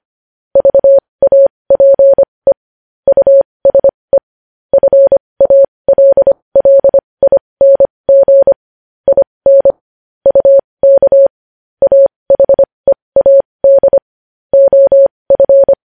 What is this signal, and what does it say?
Morse code, heard as a keyed tone, about 25 wpm, spaced as for 20 wpm, 550 Hz. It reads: VAPE USE FALLING IN UK AHEAD OF
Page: news-headlines-in-morse-code-at-20-wpm